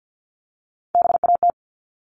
Morse code, heard as a keyed tone, about 50 wpm, 700 Hz. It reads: BAT
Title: Top 1251-1500 Words 50wpm